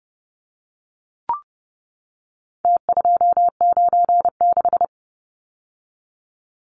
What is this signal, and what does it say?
Morse code, heard as a keyed tone, about 30 wpm, 700 Hz. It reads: T296
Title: Numbers - Contest Style - 4-Digit 30wpm